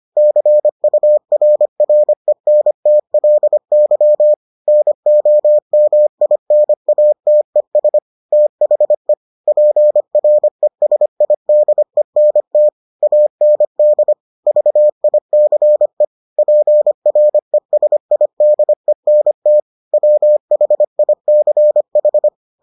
Morse code, heard as a keyed tone, about 25 wpm, 600 Hz. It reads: CURRENTLY NOMINATES THE PRESIDENT AND VICE PRESIDENT WHICH